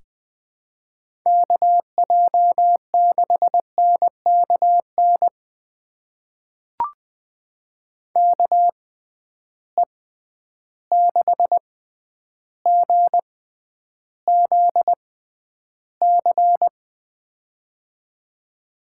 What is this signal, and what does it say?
Morse code, heard as a keyed tone, about 20 wpm, spaced as for 10 wpm, 700 Hz. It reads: KJ6NKN KE6GZC
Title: USA Call Signs - 6 Characters - 6x Character Spacing - 20wpm